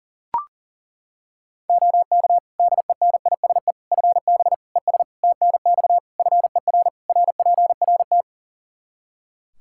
Morse code, heard as a keyed tone, about 40 wpm, 700 Hz. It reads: OK DENISE FB ES TNX FER RPRT
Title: K1USN SST QSO Practice 40wpm